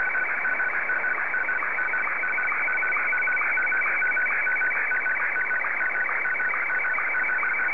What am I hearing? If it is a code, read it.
SSTV (slow-scan television), Martin 1.